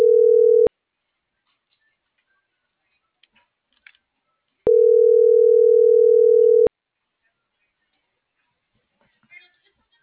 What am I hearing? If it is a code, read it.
Ringback tone (North America), 440 + 480 Hz.